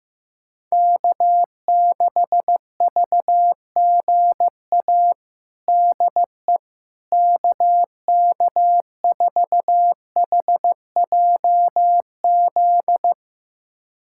Morse code, heard as a keyed tone, about 15 wpm, 700 Hz. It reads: K6VGA DE KK4HJZ
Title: USA Callsign DE Callsign 15wpm